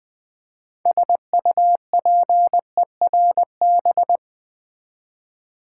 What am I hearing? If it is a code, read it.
Morse code, heard as a keyed tone, about 20 wpm, 700 Hz. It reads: SUPERB
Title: 6 Letter Words - First 3 Letters Given - 20wpm